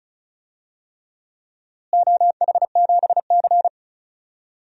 Morse code, heard as a keyed tone, about 36 wpm, 700 Hz. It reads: OH7C
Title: DXCC-100 Call Signs - 4 Characters - Repeated 3x - 35wpm